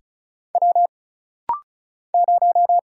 Morse code, heard as a keyed tone, about 36 wpm, 700 Hz. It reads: W 0
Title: Single Letter-Number Alt - Rapid-Fire 35wpm